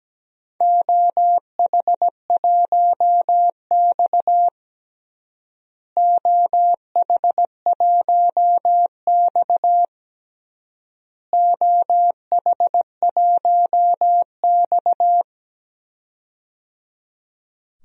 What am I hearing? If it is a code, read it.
Morse code, heard as a keyed tone, about 17 wpm, 700 Hz. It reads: OH1X OH1X OH1X
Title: DXCC-100 Call Signs - 4 Characters - Repeated 3x - 17wpm